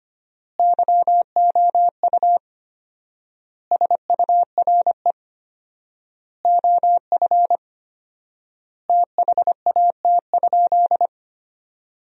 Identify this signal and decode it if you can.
Morse code, heard as a keyed tone, about 25 wpm, 700 Hz. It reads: YOU SURE OF THAT?
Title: 4x Word-Spacing - Sentences from Top 300 Words - 25wpm